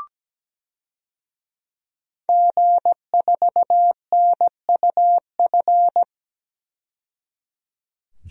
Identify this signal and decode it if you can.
Morse code, heard as a keyed tone, about 17 wpm, 700 Hz. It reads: G4NUF
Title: DXCC-100 Call Signs - 5 Characters - 17wpm